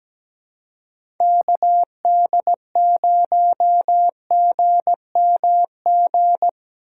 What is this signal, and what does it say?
Morse code, heard as a keyed tone, about 17 wpm, 700 Hz. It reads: KD0GMG
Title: USA Call Signs - 6 Characters - 6x Character Spacing - 17wpm